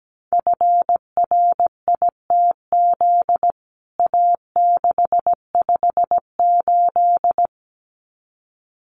Morse code, heard as a keyed tone, about 17 wpm, 700 Hz. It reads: FRITZ A658
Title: CWT Exchange 17wpm